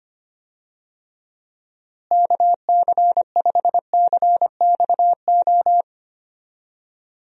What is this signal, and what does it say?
Morse code, heard as a keyed tone, about 25 wpm, 700 Hz. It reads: KC5CXO
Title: USA Call Signs - 6 Characters - Repeated 2x - 25wpm